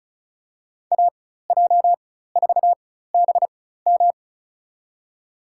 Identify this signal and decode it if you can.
Morse code, heard as a keyed tone, about 34 wpm, spaced as for 16 wpm, 700 Hz. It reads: AJ4BM
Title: USA Call Signs - 5 Characters - 4x Character Spacing - 35wpm